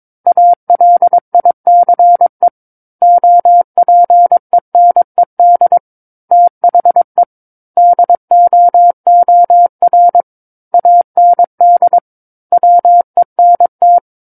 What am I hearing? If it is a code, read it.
Morse code, heard as a keyed tone, about 22 wpm, 700 Hz. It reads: ALICE OPENED THE DOOR AND WENT
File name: 06-Chapter-06 22 WPM